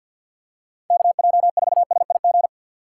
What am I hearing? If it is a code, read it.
Morse code, heard as a keyed tone, about 53 wpm, 700 Hz. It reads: KJ4SIG
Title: USA Call Signs - 6 Characters - 6x Character Spacing - 50wpm